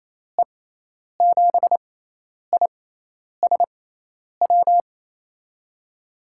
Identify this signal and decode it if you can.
Morse code, heard as a keyed tone, about 28 wpm, spaced as for 10 wpm, 700 Hz. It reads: E7ISW
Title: USA Call Signs - 6 Characters - 6x Character Spacing - 28wpm